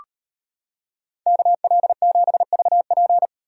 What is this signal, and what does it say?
Morse code, heard as a keyed tone, about 34 wpm, 700 Hz. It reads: KL7VP